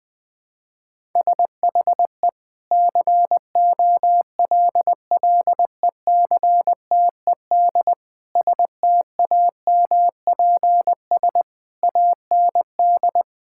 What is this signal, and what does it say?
Morse code, heard as a keyed tone, about 21 wpm, 700 Hz. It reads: SHE COLLECTED STAMPS AND
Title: Sentences with 2-Word Phrases - 20wpm